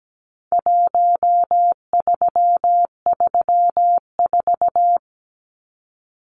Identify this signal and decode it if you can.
Morse code, heard as a keyed tone, about 17 wpm, 700 Hz. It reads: 1334
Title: Numbers - Contest Style - 4-Digit 17wpm